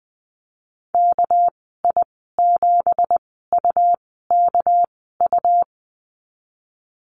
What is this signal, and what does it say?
Morse code, heard as a keyed tone, about 20 wpm, spaced as for 15 wpm, 700 Hz. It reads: KI7UKU
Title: 231 Lesson 37g - Letter x - Call Signs 20wpm